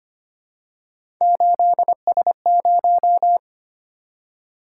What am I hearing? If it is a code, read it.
Morse code, heard as a keyed tone, about 25 wpm, 700 Hz. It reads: 8S0
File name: Three Letter-Number Combinations - 25wpm